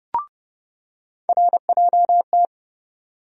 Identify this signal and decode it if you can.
Morse code, heard as a keyed tone, about 32 wpm, 700 Hz. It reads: RJT
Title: USA Call Sign Postfixes - 3 Letters 30wpm